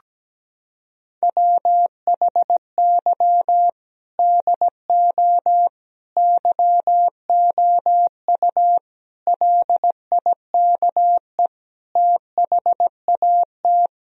Morse code, heard as a keyed tone, about 17 wpm, 700 Hz. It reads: WHY DO YOU LIKE THAT
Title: Sentences from Top 1000 Words - Review Part 4 - 17wpm